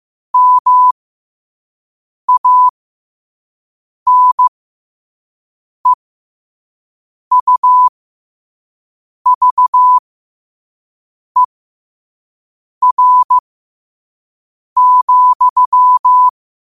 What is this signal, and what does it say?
Morse code, heard as a keyed tone, about 15 wpm, spaced as for 5 wpm, 1000 Hz. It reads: MANEUVER,
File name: qotdfiles / 5wpm